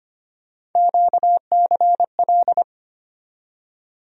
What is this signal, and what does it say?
Morse code, heard as a keyed tone, about 25 wpm, 700 Hz. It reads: QCL
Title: USA Call Sign Postfixes - 3 Letters 25wpm